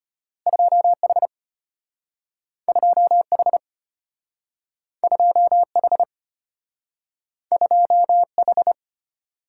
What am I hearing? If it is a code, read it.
Morse code, heard as a keyed tone, about 33 wpm, 700 Hz. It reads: 2H 2H 2H 2H